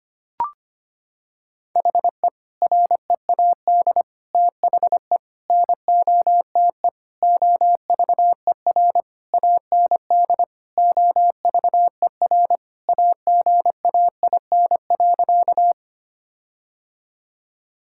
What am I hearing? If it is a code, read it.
Morse code, heard as a keyed tone, about 25 wpm, 700 Hz. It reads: HE READ THE NOTE OVER AND OVER AGAIN.
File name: practice / 25wpm / Sentences with 4-Word Phrases - 25wpm